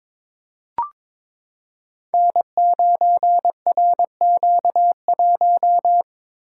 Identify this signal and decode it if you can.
Morse code, heard as a keyed tone, about 22 wpm, 700 Hz. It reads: N9RQ1
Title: Code Groups - 5 Letter:Numbers - 22wpm